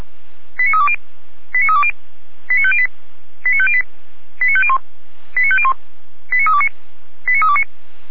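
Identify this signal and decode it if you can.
Five-tone selective call (ZVEI), heard as quick runs of five tones, tones of about 70 ms. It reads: 89320 89320 89508 89508 89501 89501 89329 89329
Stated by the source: SDR-received ZVEI five-tone call 89230, fire brigade Orlamünde.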